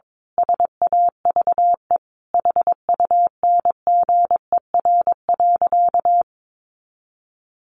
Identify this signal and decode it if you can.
Morse code, heard as a keyed tone, about 22 wpm, 700 Hz. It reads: SAVE HUNGER.
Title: A Sound Of Thunder 22wpm